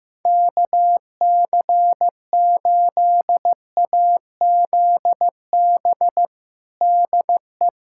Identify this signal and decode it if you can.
Morse code, heard as a keyed tone, about 15 wpm, 700 Hz. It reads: KC8AZB DE
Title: Practice QSO - Exchange 4 - Ending - 15wpm